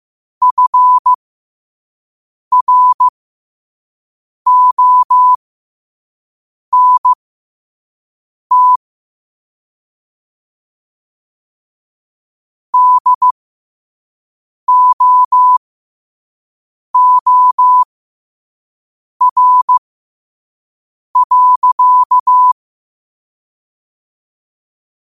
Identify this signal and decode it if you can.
Morse code, heard as a keyed tone, about 15 wpm, spaced as for 5 wpm, 1000 Hz. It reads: FRONT DOOR.
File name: qotdfiles / 5wpm